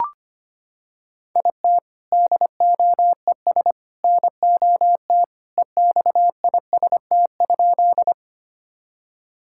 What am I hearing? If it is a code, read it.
Morse code, heard as a keyed tone, about 25 wpm, 700 Hz. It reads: IT DOES NOT EXIST?
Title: Sentences from Top 1000 Words - Review Part 5 - 25wpm